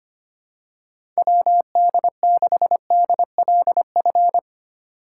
Morse code, heard as a keyed tone, about 25 wpm, 700 Hz. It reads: WD6DLF
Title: USA Call Signs - 6 Characters - 3x Character Spacing - 25wpm